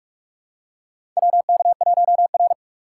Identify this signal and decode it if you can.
Morse code, heard as a keyed tone, about 46 wpm, 700 Hz. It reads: WK1R
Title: USA Call Signs - 4 Characters - 3x Character Spacing - 45wpm